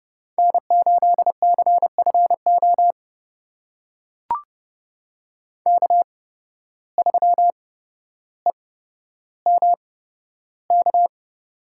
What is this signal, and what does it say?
Morse code, heard as a keyed tone, about 30 wpm, spaced as for 13 wpm, 700 Hz. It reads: N8CFO K3EMK
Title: USA Call Signs - 5 Characters - 8x Character Spacing - 30wpm